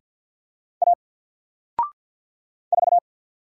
Morse code, heard as a keyed tone, about 50 wpm, 700 Hz. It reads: A 4
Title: Single Letter-Number Alt 50wpm